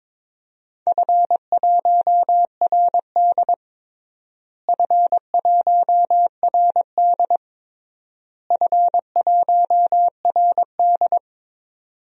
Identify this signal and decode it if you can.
Morse code, heard as a keyed tone, about 22 wpm, 700 Hz. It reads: F1RD F1RD F1RD
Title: DXCC-100 Call Signs - 4 Characters - Repeated 3x - 22wpm